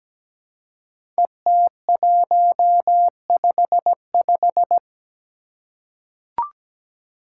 Morse code, heard as a keyed tone, about 17 wpm, 700 Hz. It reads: ET155
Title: Five Challenging Letter-Number Combinations 17wpm